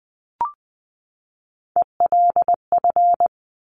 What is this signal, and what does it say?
Morse code, heard as a keyed tone, about 20 wpm, 700 Hz. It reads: ELF
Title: Top Three Letter Combinations in Words 20wpm